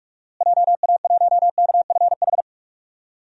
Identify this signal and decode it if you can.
Morse code, heard as a keyed tone, about 46 wpm, 700 Hz. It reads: JA1KFH
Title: DXCC-100 Call Signs - 6 Characters - Repeated 3x - 45wpm